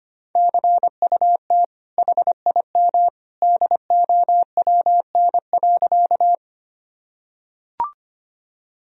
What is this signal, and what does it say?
Morse code, heard as a keyed tone, about 25 wpm, 700 Hz. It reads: CUT HIM DOWN.
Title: Sentences from Top 1000 Words - Review Part 1 - 25wpm